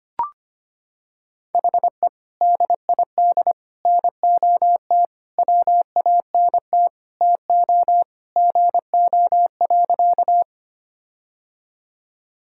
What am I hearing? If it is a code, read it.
Morse code, heard as a keyed tone, about 25 wpm, 700 Hz. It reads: HE DID NOT WANT TO GO.
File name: practice / 25wpm / Sentences with 4-Word Phrases - 25wpm